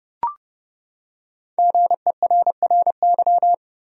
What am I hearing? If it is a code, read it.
Morse code, heard as a keyed tone, about 32 wpm, 700 Hz. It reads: GERRY
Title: USA Common Names - Max 5 Letters - 30wpm